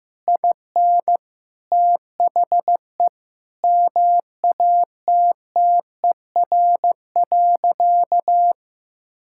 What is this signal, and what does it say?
Morse code, heard as a keyed tone, about 15 wpm, 700 Hz. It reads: IN THE MATTER.
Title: Sentences from Top 1000 Words - Review Part 6 - 15wpm